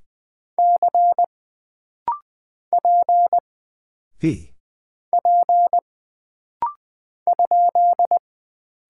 Morse code, heard as a keyed tone, about 20 wpm, 700 Hz. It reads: C P P ?
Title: Single Character (Letter:Number:Punctuation:Prosign) - Rapid-Fire - 20wpm